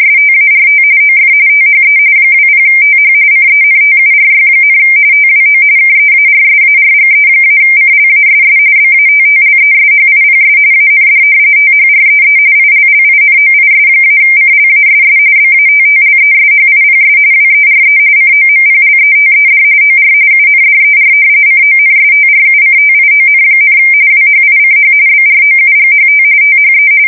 RTTY, about 45.45 baud. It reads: START OF THE 20TH CENTURY, TWO KEARSARGE-CLASS BATTLESHIPS AND THREE ILLINOIS-CLASS BATTLESHIPS WERE COMPLETED OR UNDER CONSTRUCTION, WHICH BROUGHT THE
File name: fsk170 / usnavy3